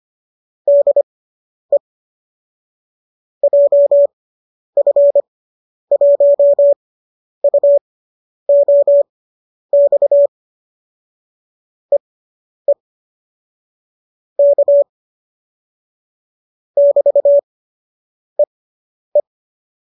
Morse code, heard as a keyed tone, about 25 wpm, spaced as for 10 wpm, 570 Hz. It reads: DE JF1UOX EE K <BT>EE